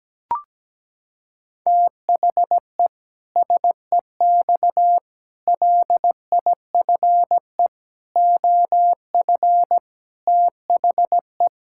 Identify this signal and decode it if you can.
Morse code, heard as a keyed tone, about 17 wpm, 700 Hz. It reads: THE SEX LIFE OF THE
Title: Pangram Challenge v2 - 17wpm